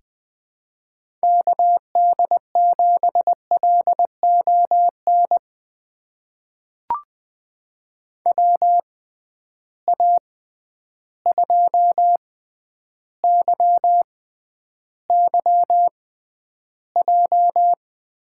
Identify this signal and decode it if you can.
Morse code, heard as a keyed tone, about 20 wpm, spaced as for 10 wpm, 700 Hz. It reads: KD7LON WA2YYJ
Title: USA Call Signs - 6 Characters - 6x Character Spacing - 20wpm